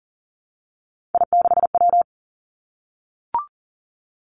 Morse code, heard as a keyed tone, about 40 wpm, 700 Hz. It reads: I6W
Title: Three Challenging Letter-Number Combinations 40wpm